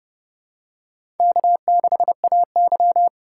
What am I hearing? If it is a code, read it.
Morse code, heard as a keyed tone, about 30 wpm, 700 Hz. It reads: K6AY
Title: USA Call Signs - 4 Characters - 4x Character Spacing - 30wpm